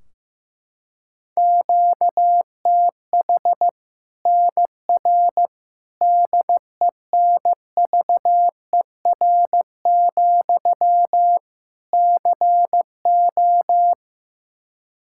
Morse code, heard as a keyed tone, about 15 wpm, 700 Hz. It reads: QTH NR DENVER, CO
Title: Practice QSO - Exchange 1 - Beginning - 15wpm